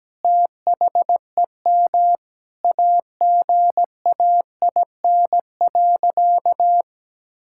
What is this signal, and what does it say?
Morse code, heard as a keyed tone, about 17 wpm, 700 Hz. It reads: THEM AGAIN.
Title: Sentences from Top 1000 Words - Review Part 6 - 17wpm